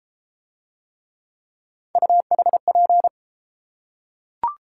Morse code, heard as a keyed tone, about 33 wpm, 700 Hz. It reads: UHP